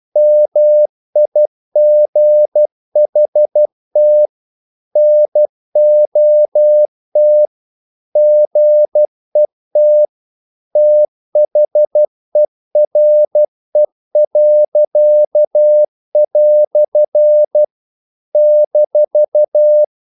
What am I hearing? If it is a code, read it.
Morse code, heard as a keyed tone, about 12 wpm, 600 Hz. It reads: MIGHT NOT GET THERE." -
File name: CW / Practice / EnglishQuotations0312